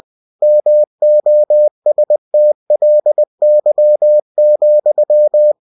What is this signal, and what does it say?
Morse code, heard as a keyed tone, about 20 wpm, 600 Hz. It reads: MOSTLY,
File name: cw-books / winnie-the-pooh / winnie-20wpm-page-0042